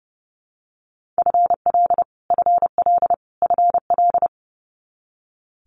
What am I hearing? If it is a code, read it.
Morse code, heard as a keyed tone, about 30 wpm, 700 Hz. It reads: FL FL FL
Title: Multi-letter Phonemes Repeated 3x 30wpm